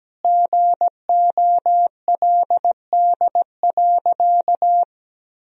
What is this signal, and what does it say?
Morse code, heard as a keyed tone, about 17 wpm, 700 Hz. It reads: GOLD.
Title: Sentences from Top 1000 Words - Review Part 3 - 17wpm